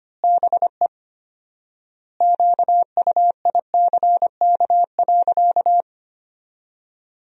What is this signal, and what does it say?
Morse code, heard as a keyed tone, about 25 wpm, 700 Hz. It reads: BE QUICK.
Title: 4x Word-Spacing - Sentences from Top 500 Words - 25wpm